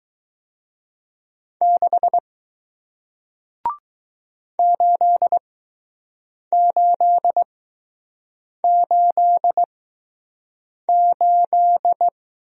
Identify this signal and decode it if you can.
Morse code, heard as a keyed tone, about 20 wpm, spaced as for 6 wpm, 700 Hz. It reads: 6 8888